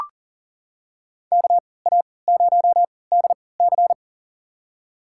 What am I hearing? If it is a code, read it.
Morse code, heard as a keyed tone, about 40 wpm, spaced as for 23 wpm, 700 Hz. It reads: KA0DC
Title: USA Call Signs - 5 Characters - 3x Character Spacing - 40wpm